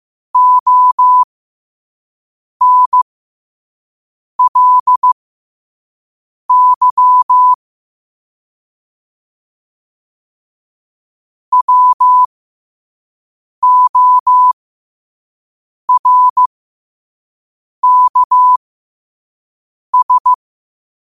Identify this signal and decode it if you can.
Morse code, heard as a keyed tone, about 15 wpm, spaced as for 5 wpm, 1000 Hz. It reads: ONLY WORKS